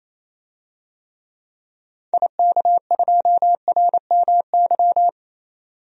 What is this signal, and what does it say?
Morse code, heard as a keyed tone, about 28 wpm, 700 Hz. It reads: IK2RMY
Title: DXCC-100 Call Signs - 6 Characters - 28wpm